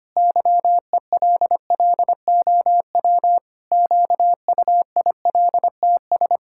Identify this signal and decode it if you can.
Morse code, heard as a keyed tone, about 25 wpm, 700 Hz. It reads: YELLOW QUILTS
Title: Pangram Challenge v2 - 25wpm